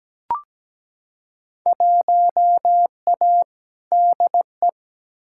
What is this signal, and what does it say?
Morse code, heard as a keyed tone, about 17 wpm, 700 Hz. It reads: 1A DE
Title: Field Day Exchanges Encore - 17wpm